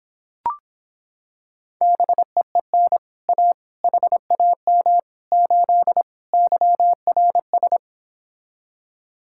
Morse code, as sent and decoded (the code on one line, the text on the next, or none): -... . -. / .- / .... .- -- / ---.. / -.-- .-. ...
BEEN A HAM 8 YRS